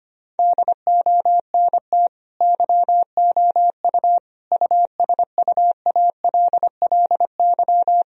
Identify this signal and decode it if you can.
Morse code, heard as a keyed tone, about 25 wpm, 700 Hz. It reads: DONT YOU USUALLY
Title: Sentences from Top 1000 Words - Review Part 3 - 25wpm